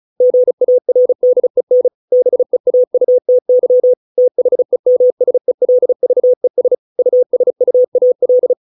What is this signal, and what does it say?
Morse code, heard as a keyed tone, about 36 wpm, 500 Hz. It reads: GARDEN BEAUTY THEMSELVES USUAL
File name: 632 words / 632 Words 35WPM